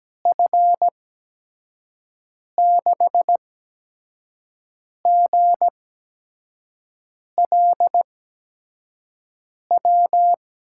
Morse code, heard as a keyed tone, about 17 wpm, spaced as for 5 wpm, 700 Hz. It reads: F6GLW
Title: USA Call Signs - 6 Characters - 8x Character Spacing - 17wpm